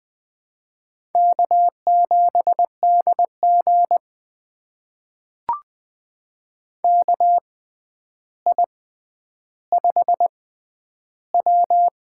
Morse code, heard as a keyed tone, about 20 wpm, spaced as for 10 wpm, 700 Hz. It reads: K7DG KI5W
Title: USA Call Signs - 4 Characters - 6x Character Spacing - 20wpm